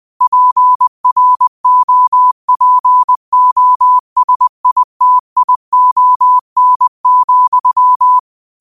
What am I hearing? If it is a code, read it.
Morse code, heard as a keyed tone, about 20 wpm, 1000 Hz. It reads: PROPOSITION,
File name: qotdfiles / 20wpm